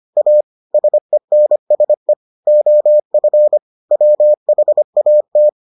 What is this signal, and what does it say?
Morse code, heard as a keyed tone, about 25 wpm, 600 Hz. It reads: A SENSE OF WHAT